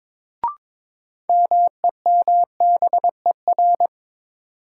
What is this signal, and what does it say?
Morse code, heard as a keyed tone, about 22 wpm, 700 Hz. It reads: MEMBER
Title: Sets of 3 Words - New Word Drill - 22wpm